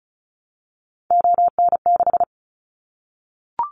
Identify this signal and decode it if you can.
Morse code, heard as a keyed tone, about 34 wpm, 700 Hz. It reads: ON6
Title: Callsign Prefixes for Top 25 DXCC - Repeated 3x - 35wpm